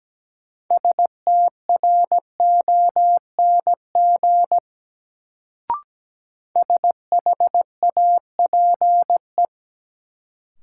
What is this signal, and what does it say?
Morse code, heard as a keyed tone, about 17 wpm, 700 Hz. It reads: STRONG SHAPE
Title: Sentences from Top 500 Words - New Word Drill - 17wpm